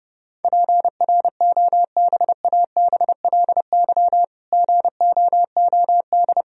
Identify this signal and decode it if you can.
Morse code, heard as a keyed tone, about 30 wpm, 700 Hz. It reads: PROBABLY GOOD
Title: Sentences from Top 1000 Words - Review Part 4 - 30wpm